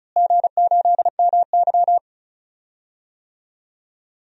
Morse code, heard as a keyed tone, about 34 wpm, 700 Hz. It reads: G8MY
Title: DXCC-100 Call Signs - 4 Characters - Repeated 3x - 35wpm